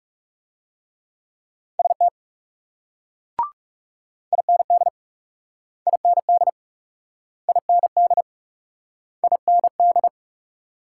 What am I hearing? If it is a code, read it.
Morse code, heard as a keyed tone, about 35 wpm, 700 Hz. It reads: ST IND IND IND IND
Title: Speed-Racing Multi-letter Phonemes - 30wpm